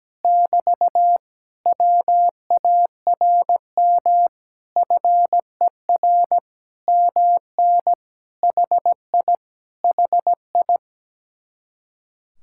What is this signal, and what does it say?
Morse code, heard as a keyed tone, about 17 wpm, 700 Hz. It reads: <BT> WARM FER MN HI HI
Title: Speed-Racing Common QSO Phrases Encore - 17wpm